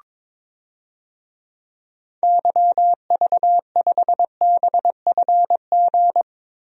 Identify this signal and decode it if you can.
Morse code, heard as a keyed tone, about 23 wpm, 700 Hz. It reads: YV5BFG
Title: DXCC-100 Call Signs - 6 Characters - 22wpm